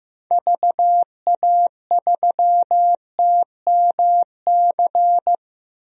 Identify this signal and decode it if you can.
Morse code, heard as a keyed tone, about 15 wpm, 700 Hz. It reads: VA3TMC
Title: DXCC-100 Call Signs - 6 Characters - Repeated 2x - 15wpm